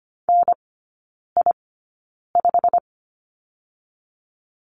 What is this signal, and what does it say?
Morse code, heard as a keyed tone, about 25 wpm, spaced as for 9 wpm, 690 Hz. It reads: NI5
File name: Lesson 15e - Number 5 - Three Characters 25wpm